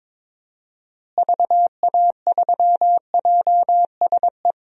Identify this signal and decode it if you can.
Morse code, heard as a keyed tone, about 21 wpm, 700 Hz. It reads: VA3JSE
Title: DXCC-100 Call Signs - 6 Characters - Repeated 3x - 22wpm